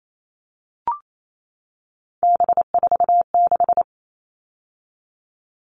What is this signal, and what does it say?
Morse code, heard as a keyed tone, about 28 wpm, 700 Hz. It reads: B46
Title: All Cut Numbers - 3-Digit 28wpm